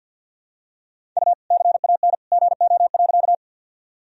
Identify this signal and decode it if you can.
Morse code, heard as a keyed tone, about 52 wpm, 700 Hz. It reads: U CAN GO.
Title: Sentences from Top 100 Words and Abbreviations 50wpm